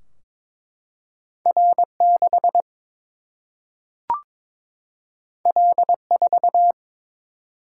Morse code, heard as a keyed tone, about 22 wpm, 700 Hz. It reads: R 6 L 4
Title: Single-Letter Single-Number 22wpm